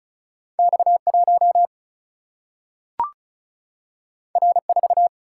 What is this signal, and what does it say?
Morse code, heard as a keyed tone, about 35 wpm, 700 Hz. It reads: X1 R4
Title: Single-Letter Single-Number 35wpm